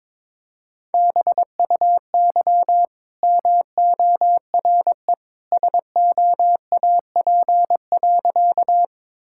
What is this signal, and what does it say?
Morse code, heard as a keyed tone, about 22 wpm, 690 Hz. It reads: BUY MORE SOAP.
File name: practice / 22wpm / Easy Sentences with Hints - 22wpm